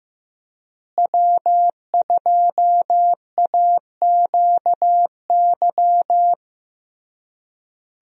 Morse code, heard as a keyed tone, about 15 wpm, 700 Hz. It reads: W2AQY
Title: USA Call Signs - 5 Characters - Repeated 3x - 15wpm